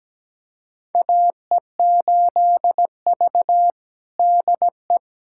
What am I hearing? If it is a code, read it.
Morse code, heard as a keyed tone, about 17 wpm, 700 Hz. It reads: AE8V DE
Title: USA Callsign DE Callsign - 4 Characters - 17wpm